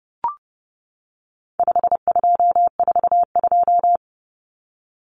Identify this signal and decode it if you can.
Morse code, heard as a keyed tone, about 30 wpm, 700 Hz. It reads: 5242
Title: Numbers - 4-Digit 30wpm